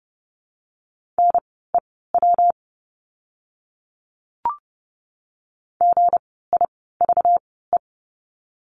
Morse code, heard as a keyed tone, about 30 wpm, spaced as for 17 wpm, 700 Hz. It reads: NEW GIVE
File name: course / 30:10wpm / 237 Lesson 38f - Letter q - Words 30wpm